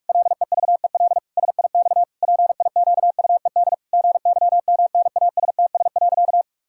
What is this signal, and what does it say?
Morse code, heard as a keyed tone, about 48 wpm, 700 Hz. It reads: LEVEL SIX PIQUED GYMNASTS.
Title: Pangram Challenge v2 - 45wpm